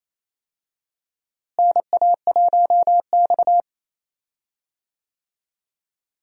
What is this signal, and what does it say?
Morse code, heard as a keyed tone, about 28 wpm, 700 Hz. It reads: NA1X
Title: USA Call Signs - 4 Characters - Repeated 2x - 28wpm